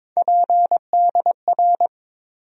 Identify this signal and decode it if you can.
Morse code, heard as a keyed tone, about 22 wpm, 700 Hz. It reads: PDR
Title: USA Call Sign Postfixes - 3 Letters 22wpm